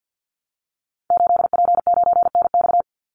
Morse code, heard as a keyed tone, about 50 wpm, 700 Hz. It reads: 8P9NX
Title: Call Signs Repeated 2x - 50wpm